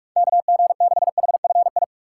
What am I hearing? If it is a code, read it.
Morse code, heard as a keyed tone, about 44 wpm, 700 Hz. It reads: KG6HFI
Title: USA Call Signs - 3x Character Spacing - 45wpm